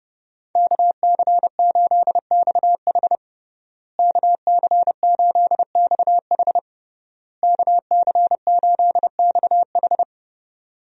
Morse code, heard as a keyed tone, about 28 wpm, 700 Hz. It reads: KC8XH KC8XH KC8XH
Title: USA Call Signs - 5 Characters - Repeated 3x - 30wpm